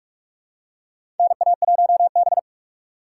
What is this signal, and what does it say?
Morse code, heard as a keyed tone, about 47 wpm, 700 Hz. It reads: NA1B
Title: USA Call Signs - 4 Characters - 3x Character Spacing - 45wpm